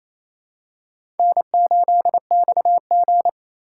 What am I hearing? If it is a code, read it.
Morse code, heard as a keyed tone, about 28 wpm, 700 Hz. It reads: N8XG
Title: USA Call Signs - 4 Characters - 8x Character Spacing - 28wpm